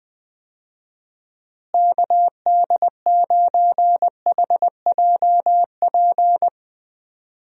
Morse code, heard as a keyed tone, about 20 wpm, 700 Hz. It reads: KD9HJP
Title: USA Call Signs - 6 Characters - Repeated 2x - 20wpm